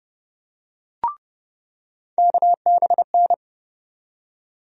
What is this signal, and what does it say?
Morse code, heard as a keyed tone, about 30 wpm, 700 Hz. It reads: KBN
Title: USA Call Sign Postfixes - 3 Letters 30wpm